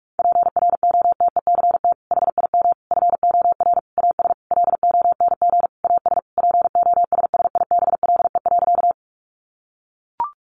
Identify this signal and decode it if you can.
Morse code, heard as a keyed tone, about 44 wpm, 700 Hz. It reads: PROTECT HIM FOR AS LONG AS POSSIBLE.
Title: Sentences from Top 900 Words 45wpm